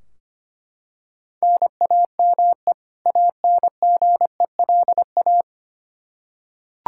Morse code, heard as a keyed tone, about 25 wpm, 700 Hz. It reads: NAME ANGELA
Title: Practice QSO - Exchange 1 - Beginning - 25wpm